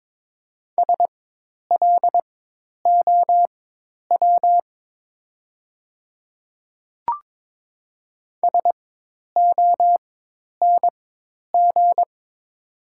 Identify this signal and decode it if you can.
Morse code, heard as a keyed tone, about 22 wpm, spaced as for 10 wpm, 700 Hz. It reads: SLOW SONG
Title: 4x Character Spacing - Top 500 Words - 22wpm